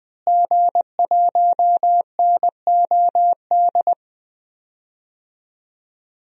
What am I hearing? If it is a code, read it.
Morse code, heard as a keyed tone, about 20 wpm, 700 Hz. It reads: G1NOD
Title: DXCC-100 Call Signs - 5 Characters - Repeated 2x - 20wpm